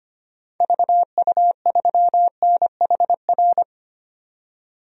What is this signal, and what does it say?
Morse code, heard as a keyed tone, about 25 wpm, 700 Hz. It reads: VU3NHR
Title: DXCC-100 Call Signs - 6 Characters - 25wpm